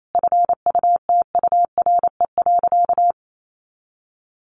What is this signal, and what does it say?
Morse code, heard as a keyed tone, about 28 wpm, 700 Hz. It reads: FUTURE.
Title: A Sound Of Thunder 28wpm